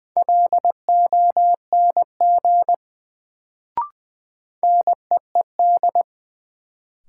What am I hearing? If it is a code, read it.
Morse code, heard as a keyed tone, about 20 wpm, 700 Hz. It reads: LONG NEED
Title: Sets of 2 Words - New Word Drill - 20wpm